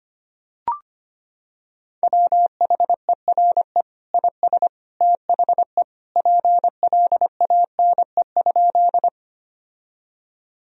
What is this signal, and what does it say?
Morse code, heard as a keyed tone, about 25 wpm, 700 Hz. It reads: WHERE IS THE PLANE?
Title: Sentences from Top 1000 Words - Review Part 3 - 25wpm